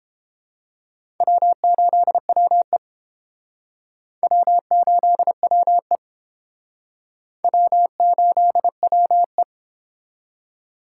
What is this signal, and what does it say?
Morse code, heard as a keyed tone, about 30 wpm, 700 Hz. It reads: W8WE W8WE W8WE